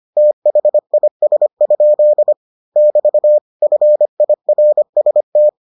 Morse code, heard as a keyed tone, about 25 wpm, 600 Hz. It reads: THIS? <BT> FIRST